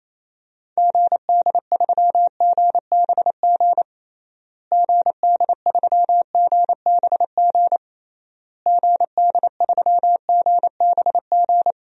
Morse code, heard as a keyed tone, about 28 wpm, 700 Hz. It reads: GD3GBG GD3GBG GD3GBG
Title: DXCC-100 Call Signs - 6 Characters - Repeated 3x - 28wpm